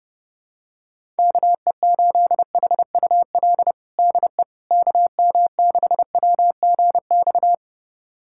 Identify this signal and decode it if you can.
Morse code, heard as a keyed tone, about 32 wpm, 700 Hz. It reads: KE8HUL DE KM6WGX
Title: USA Callsign DE Callsign 30wpm